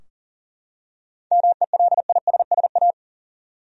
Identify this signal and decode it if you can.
Morse code, heard as a keyed tone, about 40 wpm, 700 Hz. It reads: MELISSA
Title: USA Common Names - 40wpm